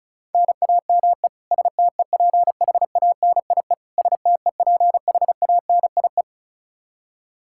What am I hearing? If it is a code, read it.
Morse code, heard as a keyed tone, about 36 wpm, 700 Hz. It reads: NAME STEPHANIE STEPHANIE
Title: Practice QSO - Exchange 1 - Beginning - 35wpm